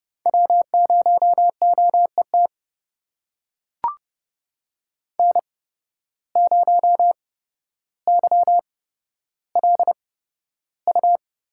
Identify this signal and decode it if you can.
Morse code, heard as a keyed tone, about 29 wpm, spaced as for 13 wpm, 700 Hz. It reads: W0OET N0YLU
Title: USA Call Signs - 5 Characters - 8x Character Spacing - 30wpm